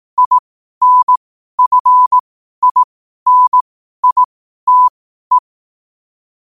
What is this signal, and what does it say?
Morse code, heard as a keyed tone, about 18 wpm, spaced as for 12 wpm, 1000 Hz. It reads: INFINITE